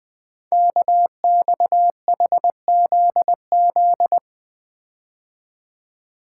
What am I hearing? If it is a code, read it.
Morse code, heard as a keyed tone, about 20 wpm, 700 Hz. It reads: KXHZZ
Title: Code Groups - 5 Letters - 20wpm